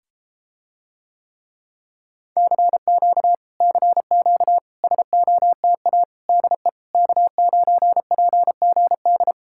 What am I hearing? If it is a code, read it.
Morse code, heard as a keyed tone, about 33 wpm, 700 Hz. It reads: CQ CQ SOTA DE K9PGD